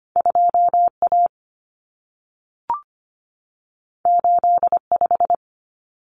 Morse code, heard as a keyed tone, about 25 wpm, 700 Hz. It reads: 2A 85
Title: Common Cut Numbers - 2-Digit 25wpm